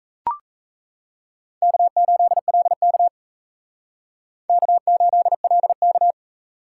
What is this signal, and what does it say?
Morse code, heard as a keyed tone, about 35 wpm, 700 Hz. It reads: K8LK K8LK
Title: Speed Racing - USA Call Signs 28wpm